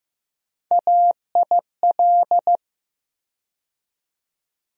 Morse code, heard as a keyed tone, about 15 wpm, 700 Hz. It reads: AIL
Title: USA Call Sign Postfixes - 3 Letters 15wpm